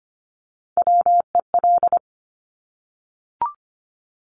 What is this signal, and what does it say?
Morse code, heard as a keyed tone, about 25 wpm, 700 Hz. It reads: WEL
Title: Top Three Letter Combinations in Words 25wpm